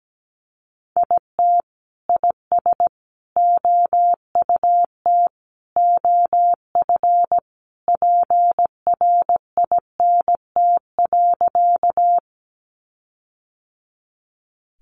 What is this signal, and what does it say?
Morse code, heard as a keyed tone, about 17 wpm, 700 Hz. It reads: IT IS OUT OF PRINT.
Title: Sentences from Top 1400 Words 17wpm